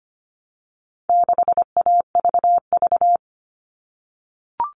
Morse code, heard as a keyed tone, about 25 wpm, 700 Hz. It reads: 6AVV
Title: All Cut Numbers - 4-Digit 25wpm